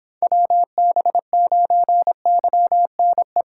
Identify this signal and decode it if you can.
Morse code, heard as a keyed tone, about 26 wpm, 700 Hz. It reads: WB9YNE